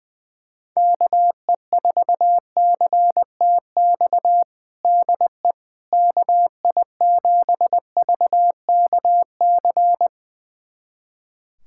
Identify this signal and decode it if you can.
Morse code, heard as a keyed tone, about 20 wpm, 700 Hz. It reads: KE4CTX DE KI7VKC
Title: USA Callsign DE Callsign - 6 Characters - 20wpm